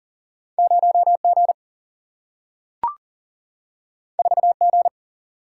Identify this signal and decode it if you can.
Morse code, heard as a keyed tone, about 39 wpm, 700 Hz. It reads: 0G 4G
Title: Single-Letter Single-Number 40wpm